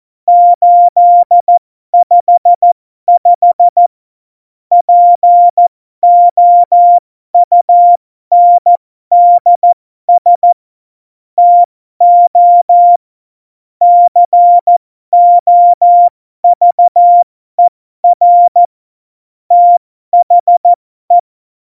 Morse code, heard as a keyed tone, about 14 wpm, spaced as for 12 wpm, 700 Hz. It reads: 855 POUNDS TO COVER THE
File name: Morsekurs / Witze